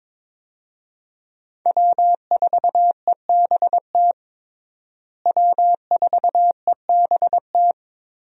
Morse code, heard as a keyed tone, about 22 wpm, 700 Hz. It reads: W4EBT W4EBT
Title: USA Call Signs - 5 Characters - Repeated 3x - 22wpm